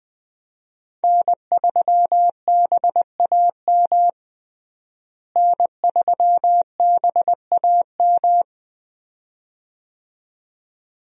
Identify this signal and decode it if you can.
Morse code, heard as a keyed tone, about 20 wpm, 700 Hz. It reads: N3BAM N3BAM
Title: USA Call Signs - 5 Characters - Repeated 2x - 20wpm